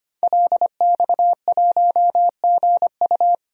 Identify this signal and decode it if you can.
Morse code, heard as a keyed tone, about 25 wpm, 700 Hz. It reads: LX1GU
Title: DXCC-100 Call Signs - 5 Characters - 25wpm